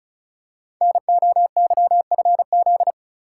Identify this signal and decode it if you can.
Morse code, heard as a keyed tone, about 36 wpm, 690 Hz. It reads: NOYFZ
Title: Code Groups - 5 Letters - 35wpm